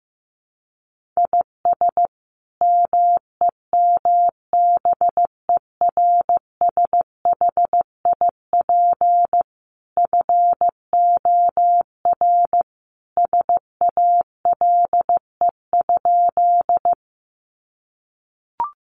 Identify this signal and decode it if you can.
Morse code, heard as a keyed tone, about 15 wpm, 700 Hz. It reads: IS MEMBERSHIP FOR SALE?
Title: Sentences from Top 1400 Words 15wpm